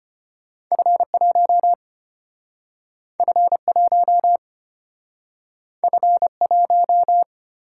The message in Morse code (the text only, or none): F1 F1 F1